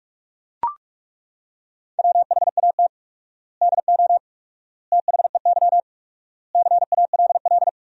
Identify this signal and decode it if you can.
Morse code, heard as a keyed tone, about 42 wpm, 700 Hz. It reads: WHAT DO THEY CALL
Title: 4x Word-Spacing - Sentences from Top 100 Words - 45wpm